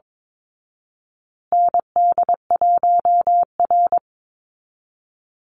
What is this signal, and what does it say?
Morse code, heard as a keyed tone, about 22 wpm, 700 Hz. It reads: ND1R
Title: Call Signs Repeated 3x - 22wpm